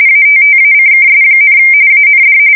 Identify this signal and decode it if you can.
RTTY, about 45.45 baud. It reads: ATTACKS, AND
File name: fsk170 / USS-New-Jersey